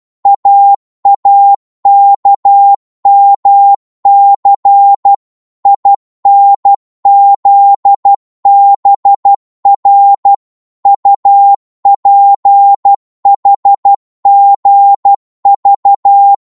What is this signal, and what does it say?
Morse code, heard as a keyed tone, about 12 wpm, 800 Hz. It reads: AAKMC INZBR UPHGV